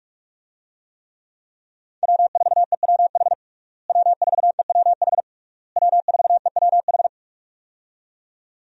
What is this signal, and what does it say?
Morse code, heard as a keyed tone, about 42 wpm, 700 Hz. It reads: W4EWH W4EWH W4EWH
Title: USA Call Signs - 5 Characters - Repeated 3x - 45wpm